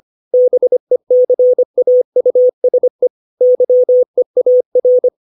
Morse code, heard as a keyed tone, about 25 wpm, 500 Hz. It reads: BECAUSE YEAR
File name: 632 words / 632 Words 25WPM